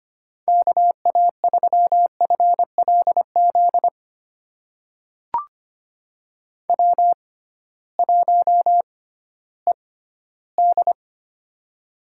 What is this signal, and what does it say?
Morse code, heard as a keyed tone, about 25 wpm, spaced as for 15 wpm, 700 Hz. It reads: KA3FLZ W1ED
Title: USA Call Signs - 6x Character Spacing - 25wpm